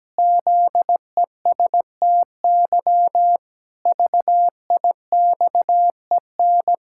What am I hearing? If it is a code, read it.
Morse code, heard as a keyed tone, about 17 wpm, 700 Hz. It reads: ZESTY VIXEN
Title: Pangram Challenge v2 - 17wpm